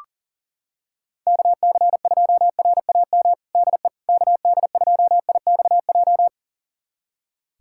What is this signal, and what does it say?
Morse code, heard as a keyed tone, about 40 wpm, 690 Hz. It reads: KC2RAM DE KD2IXJ
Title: USA Callsign DE Callsign - 6 Characters - 40wpm